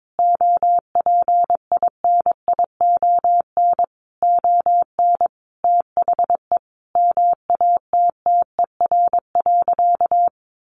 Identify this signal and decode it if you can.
Morse code, heard as a keyed tone, about 22 wpm, 700 Hz. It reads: OPINION ON THE MATTER.
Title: Sentences from Top 1200 Words 22wpm